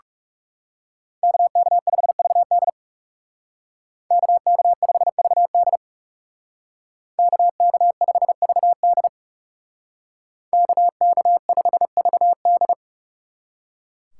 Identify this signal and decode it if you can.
Morse code, heard as a keyed tone, about 40 wpm, 700 Hz. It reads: KK5VD KK5VD KK5VD KK5VD